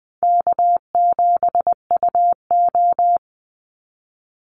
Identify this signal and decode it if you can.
Morse code, heard as a keyed tone, about 20 wpm, 700 Hz. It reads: K7UO
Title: USA Call Signs 20wpm